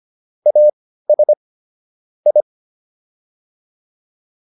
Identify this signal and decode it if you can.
Morse code, heard as a keyed tone, about 25 wpm, spaced as for 15 wpm, 600 Hz. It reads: AS I